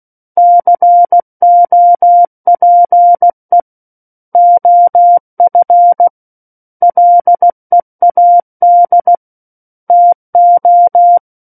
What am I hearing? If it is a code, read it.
Morse code, heard as a keyed tone, about 16 wpm, 700 Hz. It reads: COPE OF LEAD TO
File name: Book One / 07-Chapter 7 - 16 WPM